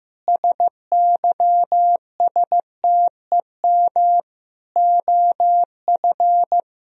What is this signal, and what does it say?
Morse code, heard as a keyed tone, about 15 wpm, 700 Hz. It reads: SYSTEM OF
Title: Sentences from Top 1000 Words - Review Part 4 - 15wpm